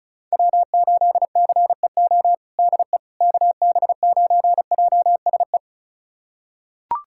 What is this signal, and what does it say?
Morse code, heard as a keyed tone, about 34 wpm, 700 Hz. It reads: W8CEO DE KB9JSE
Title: USA Callsign DE Callsign 35wpm